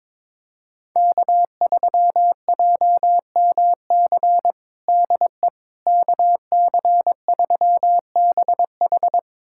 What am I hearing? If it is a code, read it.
Morse code, heard as a keyed tone, about 22 wpm, 700 Hz. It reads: K3JMC DE KC3BH
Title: USA Callsign DE Callsign - 5 Characters - 22wpm